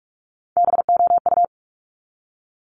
Morse code, heard as a keyed tone, about 46 wpm, 700 Hz. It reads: BOU
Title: Top Three Letter Combinations in Words 45wpm